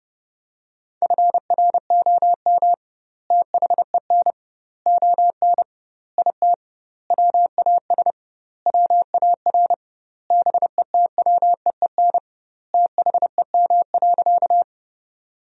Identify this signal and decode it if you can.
Morse code, heard as a keyed tone, about 30 wpm, 700 Hz. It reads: FROM THEN ON IT WAS WAR BETWEEN THEM.
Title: 2x Word-Spacing - Sentences from Top 400 Words - 30wpm